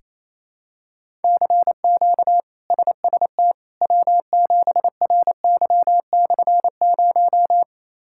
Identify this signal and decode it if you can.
Morse code, heard as a keyed tone, about 27 wpm, 700 Hz. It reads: CQ SST W7RY/0
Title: K1USN SST QSO Practice - 28wpm